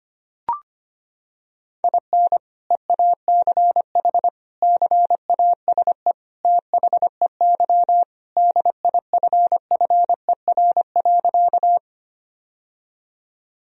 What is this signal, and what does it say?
Morse code, heard as a keyed tone, about 25 wpm, 700 Hz. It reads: IN EACH CASE THEY DIFFER.
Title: Sentences from Top 1000 Words - Review Part 1 - 25wpm